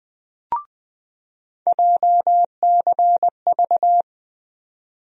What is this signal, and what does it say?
Morse code, heard as a keyed tone, about 21 wpm, 700 Hz. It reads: JCV
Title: USA Call Sign Postfixes - 3 Letters 20wpm